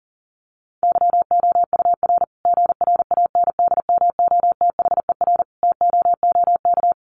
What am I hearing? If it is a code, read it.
Morse code, heard as a keyed tone, about 40 wpm, 700 Hz. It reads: YOUR GRANDMOTHER TOOK
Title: Sentences from Top 1200 Words 40wpm